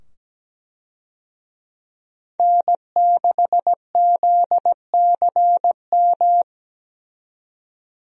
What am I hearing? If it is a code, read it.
Morse code, heard as a keyed tone, about 17 wpm, 700 Hz. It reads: N6ZCM
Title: USA Call Signs - 5 Characters - 17wpm